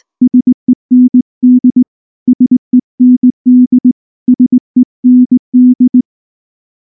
Morse code, heard as a keyed tone, about 18 wpm, 250 Hz. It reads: SEND SEND SEND
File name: morse / morse-training-20WPM-8